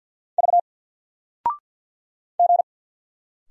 Morse code, heard as a keyed tone, about 50 wpm, 700 Hz. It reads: V G
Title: Single Letter-Number 50wpm